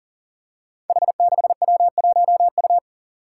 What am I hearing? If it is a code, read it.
Morse code, heard as a keyed tone, about 40 wpm, 700 Hz. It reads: H6W1U
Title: Five Challenging Letter-Number Combinations 40wpm